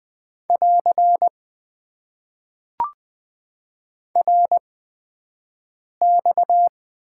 Morse code, heard as a keyed tone, about 20 wpm, spaced as for 5 wpm, 700 Hz. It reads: <AR> RX
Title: Top 90 QSO Elements - 8x Character Spacing - 20wpm